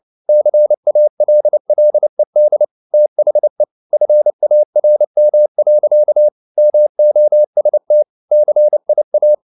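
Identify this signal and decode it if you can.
Morse code, heard as a keyed tone, about 29 wpm, 600 Hz. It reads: CALLED THE FARM. MOST CIA
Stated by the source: Morse code at 29 wpm, sending Ken Follett's novel Never in English, from CW med Gnister.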